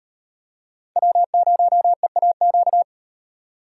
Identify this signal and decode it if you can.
Morse code, heard as a keyed tone, about 36 wpm, 700 Hz. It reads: W0EAQ